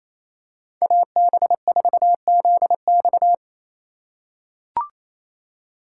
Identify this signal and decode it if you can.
Morse code, heard as a keyed tone, about 27 wpm, 700 Hz. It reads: AB4ZX
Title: USA Call Signs - 5 Characters - 4x Character Spacing - 28wpm